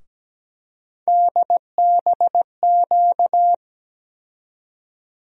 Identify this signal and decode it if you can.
Morse code, heard as a keyed tone, about 17 wpm, 700 Hz. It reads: DBQ
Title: USA Call Sign Postfixes - 3 Letters 17wpm